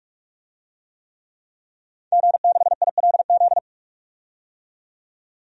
Morse code, heard as a keyed tone, about 45 wpm, 700 Hz. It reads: G6ILZ
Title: DXCC-100 Call Signs - 5 Characters - 45wpm